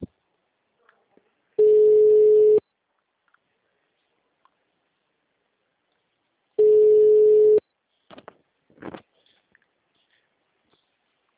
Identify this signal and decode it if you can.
Ringback tone (Europe, 425 Hz).